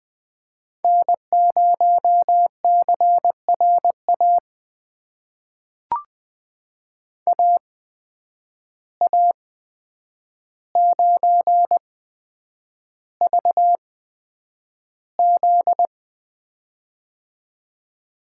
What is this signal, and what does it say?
Morse code, heard as a keyed tone, about 20 wpm, spaced as for 9 wpm, 700 Hz. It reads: N0CRA AA9VZ
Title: USA Call Signs - 5 Characters - 8x Character Spacing - 20wpm